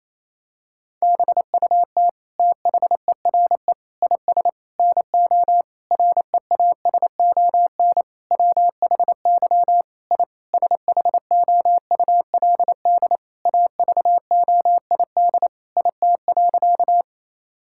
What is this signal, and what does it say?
Morse code, heard as a keyed tone, about 28 wpm, 700 Hz. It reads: BUT THERE IS NO REASON WHY I SHOULD AVOID IT.
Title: Sentences from Top 1000 Words - Review Part 6 - 28wpm